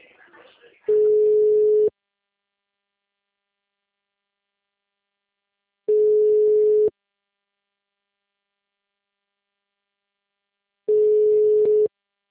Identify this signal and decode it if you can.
Ringback tone (Europe, 425 Hz), with speech.
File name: out- -2017- - - .205